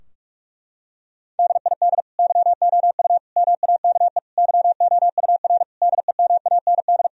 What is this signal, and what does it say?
Morse code, heard as a keyed tone, about 46 wpm, 700 Hz. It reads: DID YOU MAKE YOUR DEMAND
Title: Sentences from Top 1000 Words - Review Part 4 - 45wpm